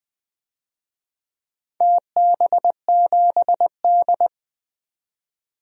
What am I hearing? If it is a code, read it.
Morse code, heard as a keyed tone, about 20 wpm, 700 Hz. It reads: TB7D
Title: DXCC-100 Call Signs - 4 Characters - 20wpm